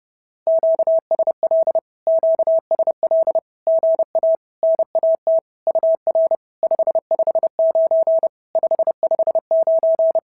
Morse code, heard as a keyed tone, about 32 wpm, 650 Hz. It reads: QSL QSL GA NAT UR 559 559
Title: SOTA QSO Practice 30wpm